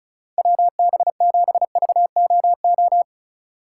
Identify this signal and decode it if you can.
Morse code, heard as a keyed tone, about 34 wpm, 700 Hz. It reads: WB7VOO
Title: USA Call Signs - 6 Characters - 4x Character Spacing - 35wpm